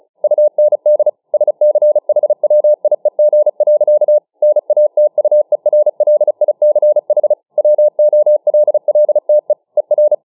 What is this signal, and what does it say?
Morse code, heard as a keyed tone, about 35 wpm, 600 Hz. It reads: UND SCHWIEG. NATUERLICH WOLLTE ER